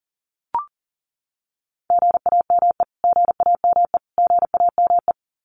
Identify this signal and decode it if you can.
Morse code, heard as a keyed tone, about 40 wpm, 700 Hz. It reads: GAME GAME GAME
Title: Top 500 Words Repeated 40wpm